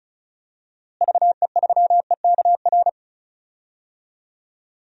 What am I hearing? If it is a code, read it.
Morse code, heard as a keyed tone, about 36 wpm, 700 Hz. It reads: VE3EKR
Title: DXCC-100 Call Signs - 6 Characters - Repeated 3x - 35wpm